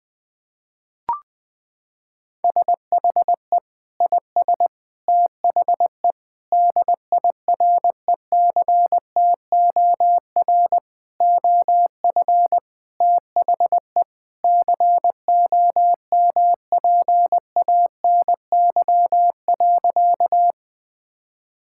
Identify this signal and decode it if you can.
Morse code, heard as a keyed tone, about 20 wpm, 700 Hz. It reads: SHE IS THE DIRECTOR OF THE COMPANY.
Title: Sentences with 3-Word Phrases - 20wpm